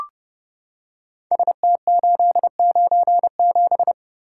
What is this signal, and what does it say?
Morse code, heard as a keyed tone, about 30 wpm, 700 Hz. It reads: ST897
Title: Code Groups - 5 Letter:Numbers - 30wpm